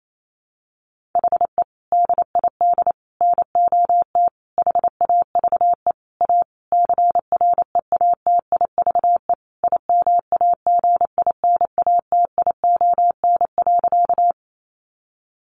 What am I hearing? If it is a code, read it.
Morse code, heard as a keyed tone, about 28 wpm, 700 Hz. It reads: HE DID NOT HAVE A CREATIVE IMAGINATION.
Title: Sentences from Top 1300 Words 28wpm